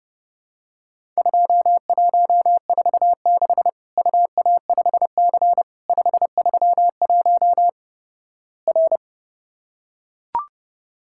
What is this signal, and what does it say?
Morse code, heard as a keyed tone, about 30 wpm, 700 Hz. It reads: 2146 UA5C 531 R
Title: Worked All Europe DX Contest - QTC Exchange - 30wpm